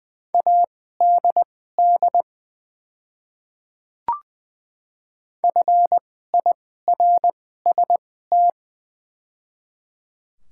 Morse code, heard as a keyed tone, about 20 wpm, spaced as for 14 wpm, 700 Hz. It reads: ADD FIRST
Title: 2x Character Spacing - Top 200 Words - 20wpm